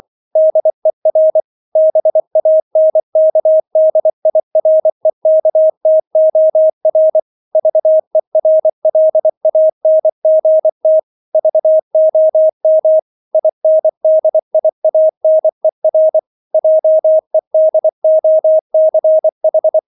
Morse code, heard as a keyed tone, about 24 wpm, 630 Hz. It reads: DER BANKDIREKTOR VERLANGT VOM INDIANER JEDOCH